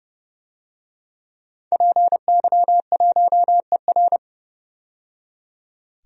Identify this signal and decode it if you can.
Morse code, heard as a keyed tone, about 32 wpm, 700 Hz. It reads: PY1ER
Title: DXCC-100 Call Signs - 5 Characters - 30wpm